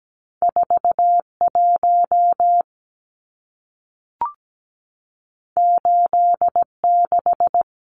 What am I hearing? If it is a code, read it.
Morse code, heard as a keyed tone, about 17 wpm, 700 Hz. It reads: 41 86
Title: Numbers - 2-Digit 17wpm